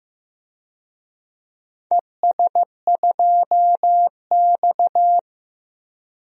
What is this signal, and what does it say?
Morse code, heard as a keyed tone, about 15 wpm, 700 Hz. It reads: ES2X
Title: DXCC-100 Call Signs - 4 Characters - 15wpm